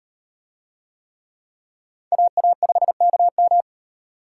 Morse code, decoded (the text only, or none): AA5KM